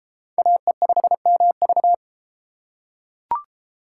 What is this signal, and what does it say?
Morse code, heard as a keyed tone, about 30 wpm, 700 Hz. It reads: AE5MV